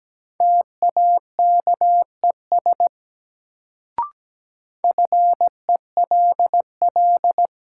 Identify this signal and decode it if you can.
Morse code, heard as a keyed tone, about 17 wpm, 700 Hz. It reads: TAKES FELL
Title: Sets of 5 Words Encore - New Word Drill - 17wpm